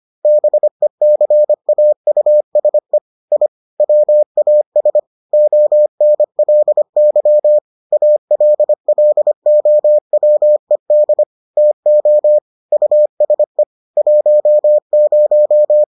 Morse code, heard as a keyed tone, about 25 wpm, 600 Hz. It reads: BECAUSE I WAS ONLY ALLOWED TO USE 10